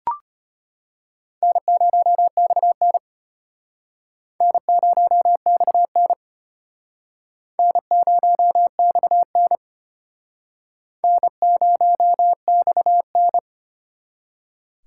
Morse code, heard as a keyed tone, about 34 wpm, 700 Hz. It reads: N0XN N0XN N0XN N0XN